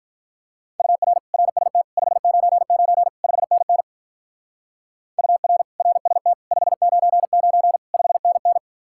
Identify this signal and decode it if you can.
Morse code, heard as a keyed tone, about 44 wpm, 700 Hz. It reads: UR RST 599 5NN UR RST 599 5NN